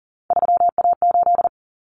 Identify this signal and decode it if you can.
Morse code, heard as a keyed tone, about 40 wpm, 700 Hz. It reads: 3A8
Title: Common Cut Numbers - 3-Digit 40wpm